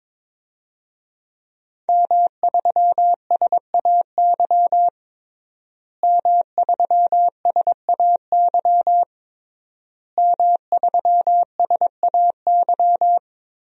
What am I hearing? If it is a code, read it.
Morse code, heard as a keyed tone, about 22 wpm, 700 Hz. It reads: M3SAY M3SAY M3SAY
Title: DXCC-100 Call Signs - 5 Characters - Repeated 3x - 22wpm